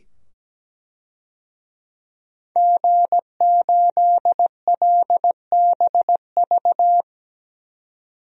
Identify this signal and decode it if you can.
Morse code, heard as a keyed tone, about 17 wpm, 700 Hz. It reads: G8LBV
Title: DXCC-100 Call Signs - 5 Characters - 17wpm